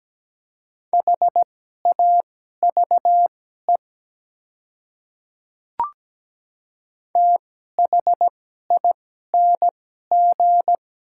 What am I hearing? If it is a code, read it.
Morse code, heard as a keyed tone, about 17 wpm, spaced as for 12 wpm, 700 Hz. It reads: HAVE THING
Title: 2x Character Spacing - Top 100 Words - 17wpm